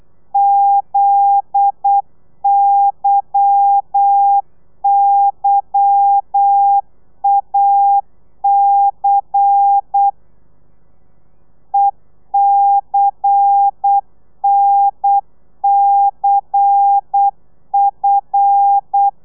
Morse code, heard as a keyed tone, about 8 wpm, 800 Hz. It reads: ZYYAC ECNCF
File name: telegrafia / i2tfj / Lez10 35